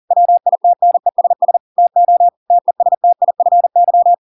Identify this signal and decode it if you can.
Morse code, heard as a keyed tone, about 40 wpm, 700 Hz. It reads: WITNESS TO TESTIFY